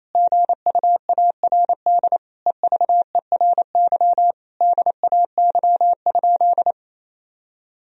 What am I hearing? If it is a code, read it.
Morse code, heard as a keyed tone, about 28 wpm, 700 Hz. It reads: GUARD EVERY DAY?
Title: Sentences from Top 1000 Words - Review Part 4 - 28wpm